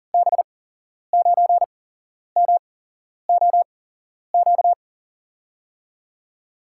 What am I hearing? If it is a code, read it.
Morse code, heard as a keyed tone, about 40 wpm, spaced as for 11 wpm, 700 Hz. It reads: B9MOQ
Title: USA Call Signs - 6 Characters - 8x Character Spacing - 40wpm